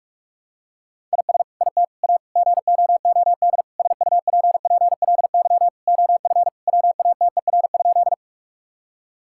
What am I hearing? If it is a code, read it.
Morse code, heard as a keyed tone, about 48 wpm, 700 Hz. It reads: IS IT A GOOD SUPPLY OF WATER?
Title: Sentences from Top 1000 Words - Review Part 5 - 45wpm